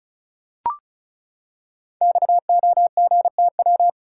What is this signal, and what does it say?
Morse code, heard as a keyed tone, about 36 wpm, 700 Hz. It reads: XOGTW
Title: Code Groups - 5 Letter:Numbers - 35wpm